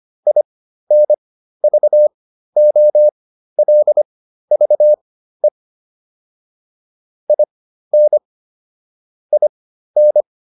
Morse code, heard as a keyed tone, about 25 wpm, spaced as for 13 wpm, 600 Hz. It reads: INVOLVE IN IN
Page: intermediate-practice-files